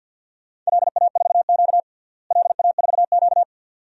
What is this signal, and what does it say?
Morse code, heard as a keyed tone, about 52 wpm, 700 Hz. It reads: LA4Q LA4Q
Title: DXCC-100 Call Signs - 4 Characters - Repeated 2x - 50wpm